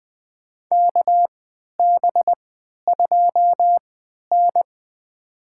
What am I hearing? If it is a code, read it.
Morse code, heard as a keyed tone, about 20 wpm, spaced as for 11 wpm, 700 Hz. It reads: KB2N
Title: USA Call Signs - 4 Characters - 3x Character Spacing - 20wpm